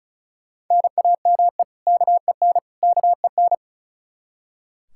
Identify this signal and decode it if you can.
Morse code, heard as a keyed tone, about 34 wpm, 700 Hz. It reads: NAME KEN KEN
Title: Practice QSO - Exchange 1 - Beginning - 35wpm